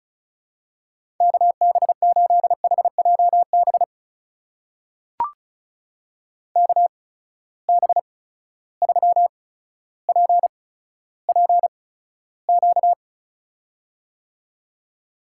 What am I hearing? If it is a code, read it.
Morse code, heard as a keyed tone, about 36 wpm, spaced as for 15 wpm, 700 Hz. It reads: KB8HJB KB3PPQ
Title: USA Call Signs - 6 Characters - 8x Character Spacing - 35wpm